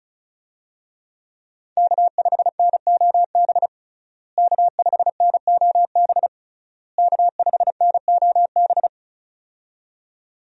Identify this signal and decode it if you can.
Morse code, heard as a keyed tone, about 36 wpm, 700 Hz. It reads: K5NOB K5NOB K5NOB
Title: USA Call Signs - 5 Characters - Repeated 3x - 35wpm